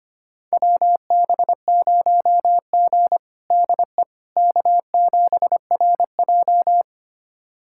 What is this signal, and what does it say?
Morse code, heard as a keyed tone, about 25 wpm, 700 Hz. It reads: WB0G DE K7RJ
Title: USA Callsign DE Callsign - 4 Characters - 25wpm